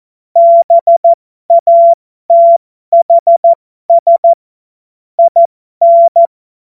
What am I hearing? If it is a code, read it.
Morse code, heard as a keyed tone, about 14 wpm, spaced as for 12 wpm, 680 Hz. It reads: BATHS IN
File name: Morsekurs / Witze